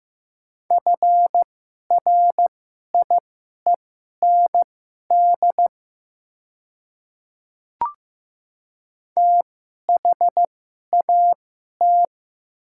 Morse code, heard as a keyed tone, about 15 wpm, spaced as for 11 wpm, 700 Hz. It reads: FRIEND THAT
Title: 2x Character Spacing - Top 300 Words - 15wpm